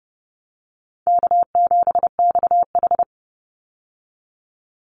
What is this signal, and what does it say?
Morse code, heard as a keyed tone, about 30 wpm, 700 Hz. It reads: K7XH
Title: Easy Call Signs Repeated 3x - 30wpm